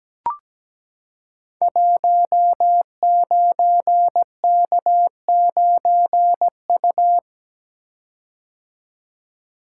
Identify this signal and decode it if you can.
Morse code, heard as a keyed tone, about 17 wpm, 700 Hz. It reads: 19K9U
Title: Code Groups - 5 Letter:Numbers - 17wpm